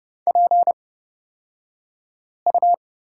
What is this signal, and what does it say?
Morse code, heard as a keyed tone, about 30 wpm, spaced as for 5 wpm, 690 Hz. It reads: PU
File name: Lesson 36d - BT - Two Characters 30wpm